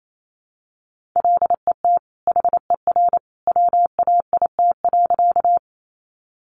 Morse code, heard as a keyed tone, about 28 wpm, 700 Hz. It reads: LET HER WAIT.
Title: Sentences from Top 500 Words 28wpm